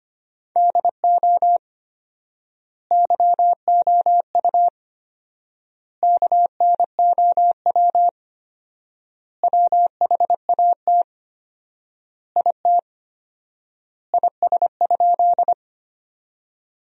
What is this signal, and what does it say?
Morse code, heard as a keyed tone, about 25 wpm, 700 Hz. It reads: DO YOU KNOW WHAT IT IS?
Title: 4x Word-Spacing - Sentences from Top 100 Words - 25wpm